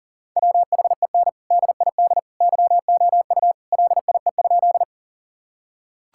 Morse code, heard as a keyed tone, about 40 wpm, 700 Hz. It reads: WHEN DID YOU LIE?
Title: Sentences from Top 1600 Words 40wpm